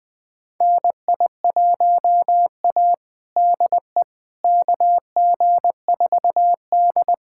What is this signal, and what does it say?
Morse code, heard as a keyed tone, about 20 wpm, 700 Hz. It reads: NI1A DE KG4D
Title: USA Callsign DE Callsign - 4 Characters - 20wpm